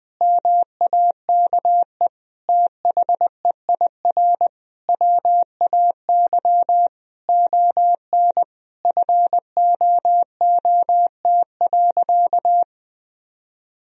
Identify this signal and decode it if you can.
Morse code, heard as a keyed tone, about 20 wpm, 700 Hz. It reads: MAKE THEIR WAY ON FOOT.
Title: Sentences from Top 1000 Words - Review Part 3 - 20wpm